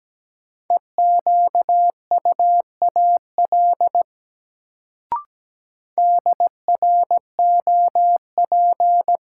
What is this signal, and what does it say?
Morse code, heard as a keyed tone, about 17 wpm, 700 Hz. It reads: EQUAL DROP
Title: Sentences from Top 1400 Words - New Word Drill Alt - 17wpm